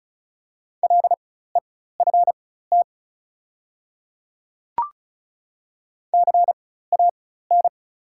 Morse code, heard as a keyed tone, about 36 wpm, spaced as for 16 wpm, 690 Hz. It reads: LEFT CAN
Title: 4x Character Spacing - Top 500 Words - 35wpm